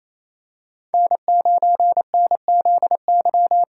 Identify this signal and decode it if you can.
Morse code, heard as a keyed tone, about 27 wpm, 700 Hz. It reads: N9NZY
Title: USA Call Signs - 5 Characters - 4x Character Spacing - 28wpm